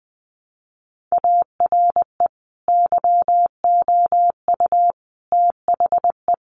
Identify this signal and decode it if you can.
Morse code, heard as a keyed tone, about 19 wpm, 700 Hz. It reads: ARE YOU THE
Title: Sentences from Top 1300 Words 20wpm